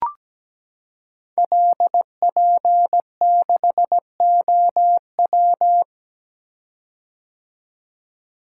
Morse code, heard as a keyed tone, about 17 wpm, 700 Hz. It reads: LP6OW
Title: Code Groups - 5 Letter:Numbers - 17wpm